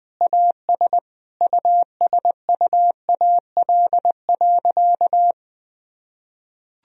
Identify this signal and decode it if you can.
Morse code, heard as a keyed tone, about 19 wpm, 700 Hz. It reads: AS USUAL.
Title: Sentences from Top 1000 Words - Review Part 2 - 20wpm